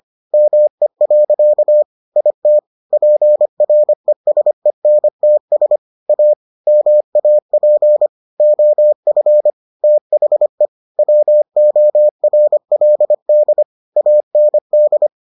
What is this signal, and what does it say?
Morse code, heard as a keyed tone, about 25 wpm, 600 Hz. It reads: ME. IT PRESENTS A MAP OF THE WORLD AND